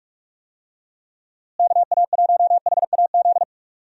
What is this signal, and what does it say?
Morse code, heard as a keyed tone, about 46 wpm, 700 Hz. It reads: KA1HAZ
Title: USA Call Signs - 6 Characters - Repeated 2x - 45wpm